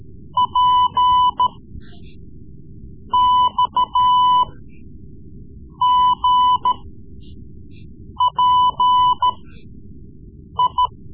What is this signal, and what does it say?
Morse code, heard as a keyed tone, about 14 wpm, spaced as for 5 wpm, 1000 Hz. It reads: PXGPI
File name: Morse Lessons / Morse Lessons / Tape-3B